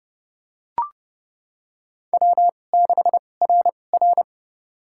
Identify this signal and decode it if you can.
Morse code, heard as a keyed tone, about 30 wpm, spaced as for 22 wpm, 700 Hz. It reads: W6RR
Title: USA Call Signs - 4 Characters - 2x Character Spacing - 30wpm